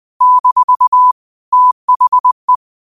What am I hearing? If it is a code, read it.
Morse code, heard as a keyed tone, about 20 wpm, 1000 Hz. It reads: - THE